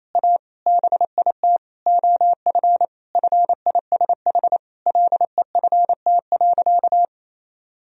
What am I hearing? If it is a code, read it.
Morse code, heard as a keyed tone, about 28 wpm, 700 Hz. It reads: A BIT OF FISH LEFT.
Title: Sentences from Top 1000 Words - Review Part 1 - 28wpm